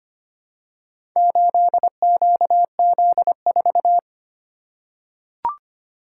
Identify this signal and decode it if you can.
Morse code, heard as a keyed tone, about 25 wpm, 700 Hz. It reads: 8QZ4
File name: practice / 25wpm / Four Letter-Number Combinations - 25wpm